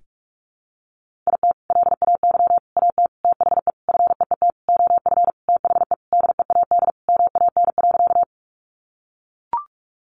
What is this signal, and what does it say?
Morse code, heard as a keyed tone, about 44 wpm, 700 Hz. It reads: IT LAY AT THE FEET OF THE DEAD MAN.
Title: Sentences from Top 1500 Words 45wpm